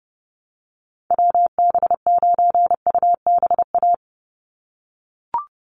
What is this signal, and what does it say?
Morse code, heard as a keyed tone, about 30 wpm, 700 Hz. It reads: WB9UBA
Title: USA Call Signs 30wpm